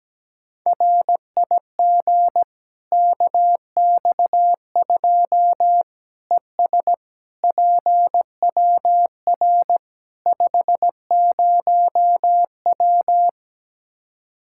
Morse code, heard as a keyed tone, about 17 wpm, 700 Hz. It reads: RIG KX2 ES PWR 50W
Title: Practice QSO - Exchange 2 - Info - 17wpm